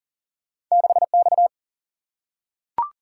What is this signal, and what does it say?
Morse code, heard as a keyed tone, about 42 wpm, 700 Hz. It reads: 6X
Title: Single-Letter Single-Number 40wpm